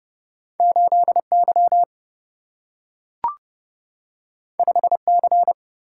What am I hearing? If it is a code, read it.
Morse code, heard as a keyed tone, about 32 wpm, 700 Hz. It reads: 8 Y 5 C
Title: Single-Letter Single-Number 30wpm